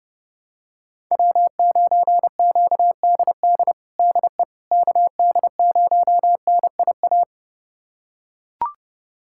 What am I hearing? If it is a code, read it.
Morse code, heard as a keyed tone, about 30 wpm, 700 Hz. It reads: W9QDD DE KD0NIA
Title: USA Callsign DE Callsign 30wpm